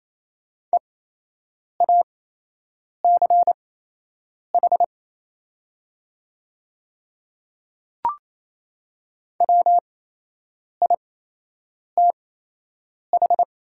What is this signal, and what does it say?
Morse code, heard as a keyed tone, about 28 wpm, spaced as for 8 wpm, 690 Hz. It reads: EACH WITH
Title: 8x Character Spacing - Top 100 Words - 28wpm